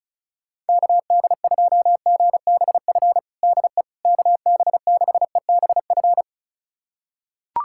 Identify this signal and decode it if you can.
Morse code, heard as a keyed tone, about 36 wpm, 700 Hz. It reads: KD2GBF DE KB6EBF
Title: USA Callsign DE Callsign 35wpm